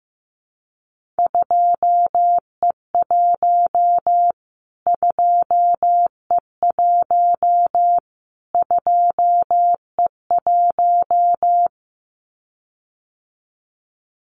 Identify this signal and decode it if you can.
Morse code, heard as a keyed tone, about 15 wpm, 700 Hz. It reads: 2E1 2E1 2E1
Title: Callsign Prefixes for Top 25 DXCC - Repeated 3x - 15wpm